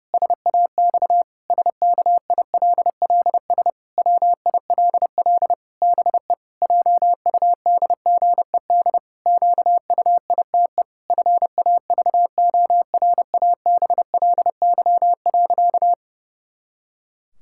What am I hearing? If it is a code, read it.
Morse code, heard as a keyed tone, about 32 wpm, 700 Hz. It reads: SAX SKILLS WILL BE JUDGED QUITE FAVORABLY.
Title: Pangram Challenge v2 - 30wpm